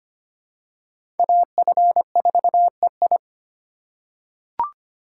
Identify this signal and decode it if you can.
Morse code, heard as a keyed tone, about 25 wpm, 700 Hz. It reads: AF4EI
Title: USA Call Signs - 5 Characters - 2x Character Spacing - 25wpm